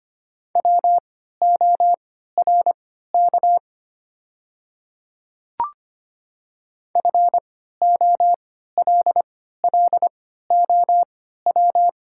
Morse code, heard as a keyed tone, about 25 wpm, spaced as for 14 wpm, 700 Hz. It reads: WORK FOLLOW
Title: 3x Character Spacing - Top 300 Words - 25wpm